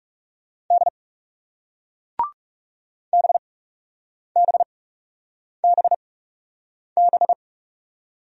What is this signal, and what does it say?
Morse code, heard as a keyed tone, about 42 wpm, spaced as for 8 wpm, 700 Hz. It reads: D BBBB